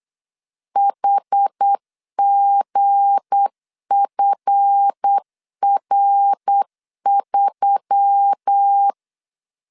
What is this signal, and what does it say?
Morse code, heard as a keyed tone, about 8 wpm, 800 Hz. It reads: HGFR3
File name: LEZIONI / 7 GF9Z / 7 GF9Z 8wpm 3213